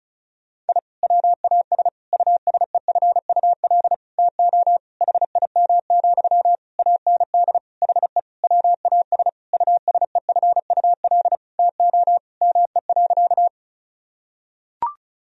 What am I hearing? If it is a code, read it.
Morse code, heard as a keyed tone, about 36 wpm, 700 Hz. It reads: I WAS USEFUL TO HIM, AND HE WAS USEFUL TO ME.
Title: Sentences from Top 1000 Words - Review Part 4 - 35wpm